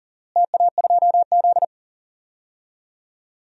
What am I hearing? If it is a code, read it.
Morse code, heard as a keyed tone, about 40 wpm, 700 Hz. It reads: TA2Z
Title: DXCC-100 Call Signs - 4 Characters - Repeated 3x - 40wpm